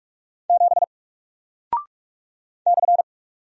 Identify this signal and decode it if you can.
Morse code, heard as a keyed tone, about 44 wpm, 700 Hz. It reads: 7 /
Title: Single Character (Letter:Number:Punctuation:Prosign) Alt 45wpm